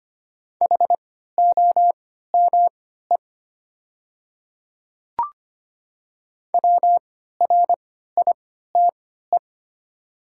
Morse code, heard as a keyed tone, about 25 wpm, spaced as for 14 wpm, 700 Hz. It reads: HOME WRITE
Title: 3x Character Spacing - Top 200 Words - 25wpm